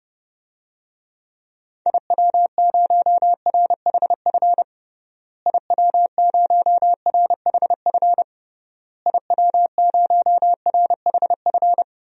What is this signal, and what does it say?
Morse code, heard as a keyed tone, about 30 wpm, 700 Hz. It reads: IW0RHF IW0RHF IW0RHF
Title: DXCC-100 Call Signs - 6 Characters - Repeated 3x - 30wpm